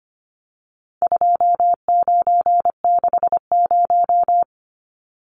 Morse code, heard as a keyed tone, about 25 wpm, 700 Hz. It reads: 2960
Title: Numbers - 4-Digit 25wpm